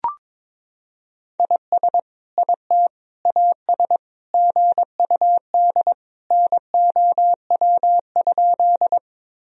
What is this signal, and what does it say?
Morse code, heard as a keyed tone, about 22 wpm, 700 Hz. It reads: IS IT AS GUD NOW?
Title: Sentences from Top 200 Words and Abbreviations 22wpm